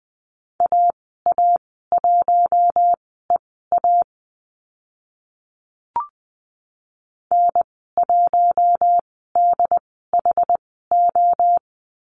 Morse code, heard as a keyed tone, about 20 wpm, spaced as for 14 wpm, 700 Hz. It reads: AA1EA N1DHO
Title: 069 Lesson 12g - Letter d - Call Signs 20wpm